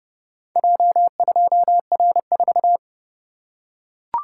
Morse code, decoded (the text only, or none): J2R4